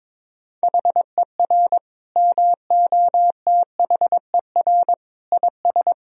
Morse code, heard as a keyed tone, about 21 wpm, 700 Hz. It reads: HER MOTHER IS
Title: Sentences from Top 1000 Words - Review Part 2 - 22wpm